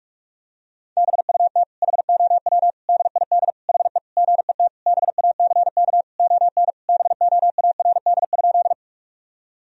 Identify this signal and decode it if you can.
Morse code, heard as a keyed tone, about 46 wpm, 700 Hz. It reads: BUT HOW DID HE GET BACK ON BOARD?
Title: Sentences from Top 1000 Words - Review Part 4 - 45wpm